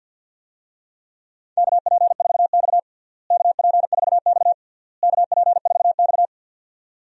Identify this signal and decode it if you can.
Morse code, heard as a keyed tone, about 51 wpm, 700 Hz. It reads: KP4X KP4X KP4X
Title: DXCC-100 Call Signs - 4 Characters - Repeated 3x - 50wpm